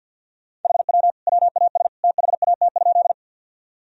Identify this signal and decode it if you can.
Morse code, heard as a keyed tone, about 45 wpm, 700 Hz. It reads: HW WAS THAT?
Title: Sentences from Top 100 Words and Abbreviations 50wpm